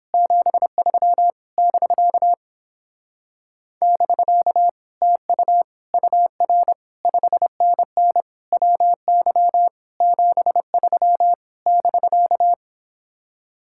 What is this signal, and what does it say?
Morse code, heard as a keyed tone, about 29 wpm, 700 Hz. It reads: 73 <BK> <BK> TU UR 5NN WY 73 <BK>